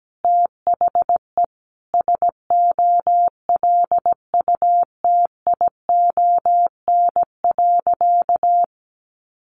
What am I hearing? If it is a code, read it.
Morse code, heard as a keyed tone, about 17 wpm, 700 Hz. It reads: THE SOLUTION.
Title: Sentences from Top 1300 Words 17wpm